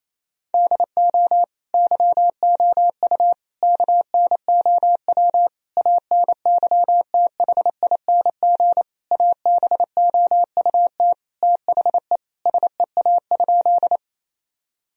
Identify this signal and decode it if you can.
Morse code, heard as a keyed tone, about 27 wpm, 700 Hz. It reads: DO YOU KNOW ANYTHING ABOUT THE SEA?
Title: Sentences from Top 1000 Words - Review Part 1 - 28wpm